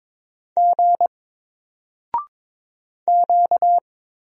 Morse code, heard as a keyed tone, about 21 wpm, 700 Hz. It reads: G Q
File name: Single-Letter-Number-22wpm